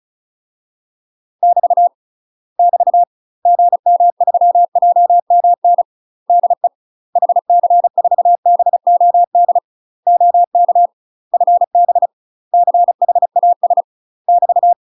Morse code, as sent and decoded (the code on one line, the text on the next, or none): -...- / -...- / --. -- ...-- .--- -- -. / -.. / .... -.-. ....- -... --- -.. / --- -.- / ..-. -... / -.-. .... .- ... / -...-
<BT> <BT> GM3JMN DE HC4BOD OK FB CHAS <BT>